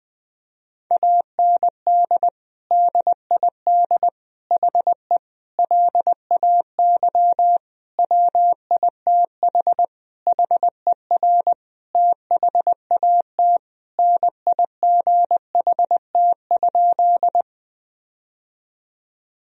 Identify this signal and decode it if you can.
Morse code, heard as a keyed tone, about 20 wpm, 700 Hz. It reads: AND DID HE LAY WITH HER THAT NIGHT?
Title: Sentences from Top 1000 Words - Review Part 2 - 20wpm